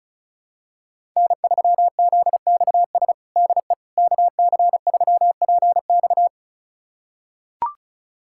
Morse code, heard as a keyed tone, about 36 wpm, 700 Hz. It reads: N3ZXS DE KC3PX
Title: USA Callsign DE Callsign - 5 Characters - 35wpm